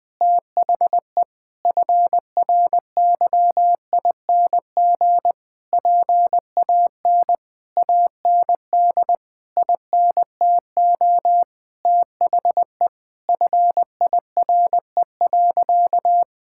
Morse code, heard as a keyed tone, about 20 wpm, 700 Hz. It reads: THE FRYING PAN AND INTO THE FIRE.
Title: Sentences with 9-Word Phrases - Spoken Beginning - 20wpm